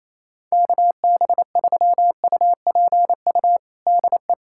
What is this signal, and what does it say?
Morse code, heard as a keyed tone, about 28 wpm, 700 Hz. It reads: KB3UPU DE
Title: USA Callsign DE Callsign 28wpm